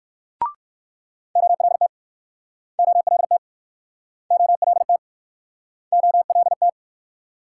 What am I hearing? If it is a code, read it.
Morse code, heard as a keyed tone, about 61 wpm, 700 Hz. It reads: OLT OLT OLT OLT